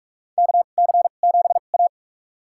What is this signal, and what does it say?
Morse code, heard as a keyed tone, about 46 wpm, spaced as for 32 wpm, 700 Hz. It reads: KC7A
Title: USA Call Signs - 4 Characters - 2x Character Spacing - 45wpm